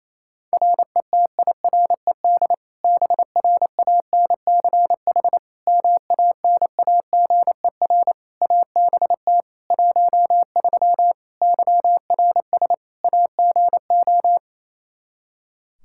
Morse code, heard as a keyed tone, about 28 wpm, 700 Hz. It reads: RETIRED BRANCH MANAGER ABT 13 YRS AGO
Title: K1USN SST QSO Practice 28wpm